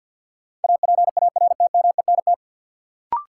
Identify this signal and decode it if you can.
Morse code, heard as a keyed tone, about 51 wpm, 700 Hz. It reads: APARTMENT
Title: Sentences from Top 900 Words - New Word Drill - 50wpm